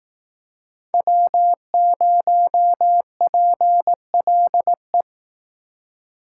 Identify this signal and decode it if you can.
Morse code, heard as a keyed tone, about 18 wpm, 700 Hz. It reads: W0PLE
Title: Speed Racing - USA 1x3 Call Signs 15wpm